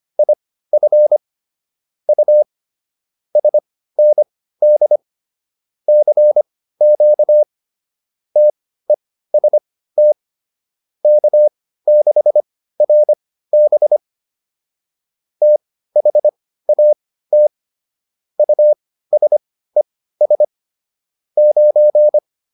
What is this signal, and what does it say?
Morse code, heard as a keyed tone, about 25 wpm, spaced as for 15 wpm, 600 Hz. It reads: IF U SND CQ TEST K6RB THAT USES 9